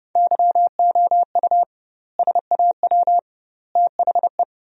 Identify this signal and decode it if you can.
Morse code, heard as a keyed tone, about 31 wpm, 700 Hz. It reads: YOU SAW THE
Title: 2x Word-Spacing - Sentences from Top 400 Words - 30wpm